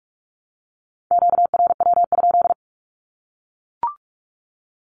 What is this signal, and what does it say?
Morse code, heard as a keyed tone, about 44 wpm, 700 Hz. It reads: QRW?
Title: Q-Codes Repeated 3x 45wpm